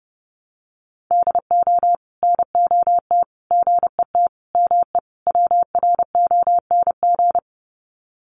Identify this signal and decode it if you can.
Morse code, heard as a keyed tone, about 30 wpm, 700 Hz. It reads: DO NOT GET ME WRONG
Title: Sets of 5 Words Encore 30wpm